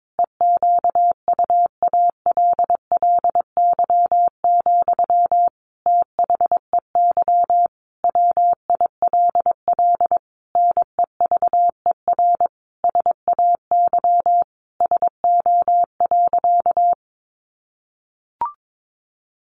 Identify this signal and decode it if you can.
Morse code, heard as a keyed tone, about 22 wpm, 700 Hz. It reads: EQUALLY, THEY WILL NEVER SAY SO.
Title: Sentences from Top 1400 Words 22wpm